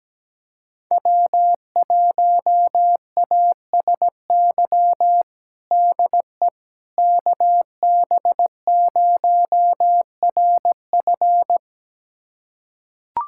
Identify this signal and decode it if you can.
Morse code, heard as a keyed tone, about 17 wpm, 700 Hz. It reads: W1ASY DE KB0RF
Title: USA Callsign DE Callsign - 5 Characters - 17wpm